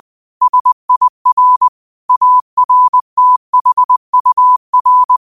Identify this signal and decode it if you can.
Morse code, heard as a keyed tone, about 20 wpm, 1000 Hz. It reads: SIR ARTHUR
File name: qotdfiles / 20wpm